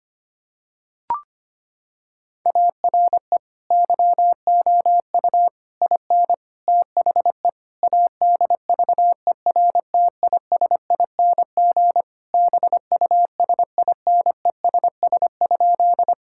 Morse code, heard as a keyed tone, about 25 wpm, 700 Hz. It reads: ARE YOU IN THE ADVERTISING BUSINESS?
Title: Sentences from Top 1000 Words - Review Part 6 - 25wpm